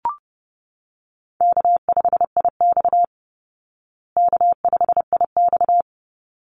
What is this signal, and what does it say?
Morse code, heard as a keyed tone, about 30 wpm, 700 Hz. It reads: K5IX K5IX
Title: Easy Call Signs Repeated 2x - 30wpm